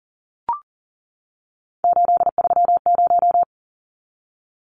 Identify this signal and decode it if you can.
Morse code, heard as a keyed tone, about 40 wpm, 700 Hz. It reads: 830
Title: Numbers - 3-Digit 40wpm